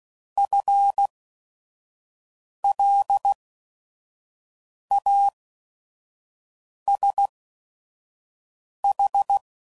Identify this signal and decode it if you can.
Morse code, heard as a keyed tone, about 16 wpm, spaced as for 5 wpm, 800 Hz. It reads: FLASH